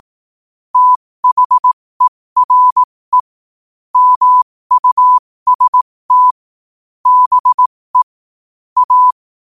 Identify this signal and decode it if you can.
Morse code, heard as a keyed tone, about 18 wpm, spaced as for 15 wpm, 1000 Hz. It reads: THERE MUST BE A
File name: qotdfiles / 15wpm